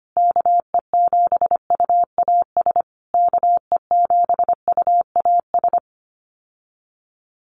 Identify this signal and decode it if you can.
Morse code, heard as a keyed tone, about 25 wpm, 700 Hz. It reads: KE7UAS KE7UAS
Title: USA Call Signs Repeated 3x 25wpm